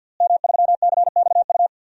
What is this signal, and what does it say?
Morse code, heard as a keyed tone, about 52 wpm, 700 Hz. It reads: M3CXU
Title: DXCC-100 Call Signs - 5 Characters - Repeated 3x - 50wpm